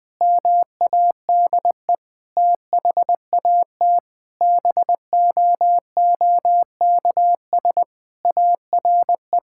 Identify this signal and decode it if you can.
Morse code, heard as a keyed tone, about 20 wpm, 700 Hz. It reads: MADE THAT BOOKS ARE
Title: Sentences with 8-Word Phrases - 20wpm